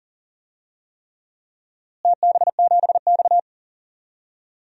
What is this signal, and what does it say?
Morse code, heard as a keyed tone, about 40 wpm, 700 Hz. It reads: TB7X
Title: DXCC-100 Call Signs - 4 Characters - Repeated 3x - 40wpm